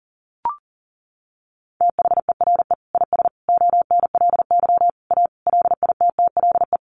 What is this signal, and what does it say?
Morse code, heard as a keyed tone, about 40 wpm, 700 Hz. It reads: THERE IS ONLY A LITTLE
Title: Sentences from Top 1500 Words 40wpm